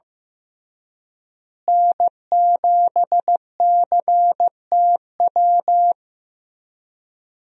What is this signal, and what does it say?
Morse code, heard as a keyed tone, about 15 wpm, 700 Hz. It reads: N7CTW
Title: USA Call Signs - 5 Characters - Repeated 3x - 15wpm